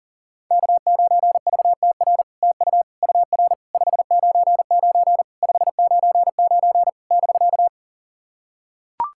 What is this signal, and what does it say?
Morse code, heard as a keyed tone, about 40 wpm, 700 Hz. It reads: K9VTR TU UR 599 599 <BK>
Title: SOTA QSO Practice 40wpm